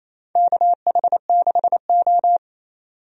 Morse code, heard as a keyed tone, about 27 wpm, 690 Hz. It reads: KH6O
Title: DXCC-100 Call Signs - 4 Characters - 28wpm